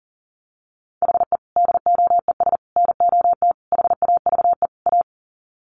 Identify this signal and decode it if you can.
Morse code, heard as a keyed tone, about 40 wpm, 700 Hz. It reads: HE DOES NOT HAVE A
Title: Sets of 5 Words Encore 40wpm